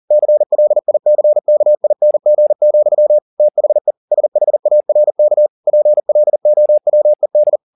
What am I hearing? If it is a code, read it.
Morse code, heard as a keyed tone, about 40 wpm, 600 Hz. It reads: CLICKING, THE SHARK PLOWED
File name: the old man and the sea 2 40wpm